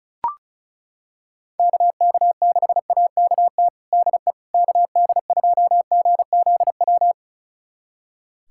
Morse code, heard as a keyed tone, about 36 wpm, 690 Hz. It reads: KK6AKT DE KD2GZW
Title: USA Callsign DE Callsign - 6 Characters - 35wpm